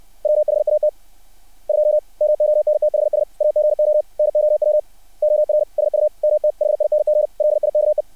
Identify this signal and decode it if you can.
Morse code, heard as a keyed tone, about 60 wpm, 600 Hz. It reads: WHAT 2 MONTHS AGO NOW OR SUMTHING LIKE
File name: archived / CRSnet